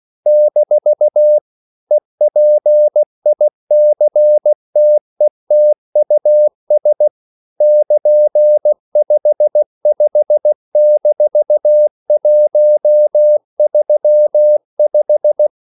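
Morse code, heard as a keyed tone, about 16 wpm, 600 Hz. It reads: - EPICTETUS <KN>55-135
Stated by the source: English quotations sent in CW as practice at 16 words per minute.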